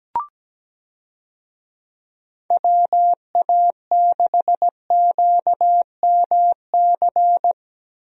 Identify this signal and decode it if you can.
Morse code, heard as a keyed tone, about 17 wpm, 700 Hz. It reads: WA6QMC
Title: USA Call Signs - 6 Characters - Repeated 2x - 17wpm